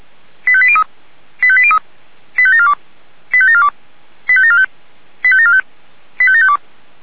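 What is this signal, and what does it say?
Five-tone selective call (ZVEI), heard as quick runs of five tones, tones of about 70 ms. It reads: 85803 85803 85742 85742 85745 85745 85752
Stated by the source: Five-tone ZVEI selective call received on SDR.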